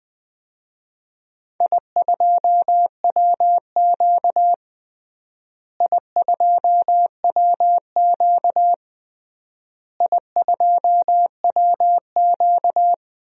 Morse code, heard as a keyed tone, about 20 wpm, 700 Hz. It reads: I2WQ I2WQ I2WQ
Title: DXCC-100 Call Signs - 4 Characters - Repeated 3x - 20wpm